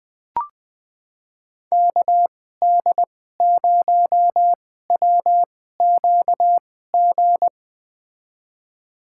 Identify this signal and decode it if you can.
Morse code, heard as a keyed tone, about 20 wpm, spaced as for 15 wpm, 700 Hz. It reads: KD0WQG
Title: USA Call Signs - 6 Characters - 2x Character Spacing - 20wpm